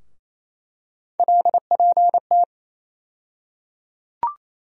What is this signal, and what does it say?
Morse code, heard as a keyed tone, about 28 wpm, 700 Hz. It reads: LPT
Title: USA Call Sign Postfixes - 3 Letters 28wpm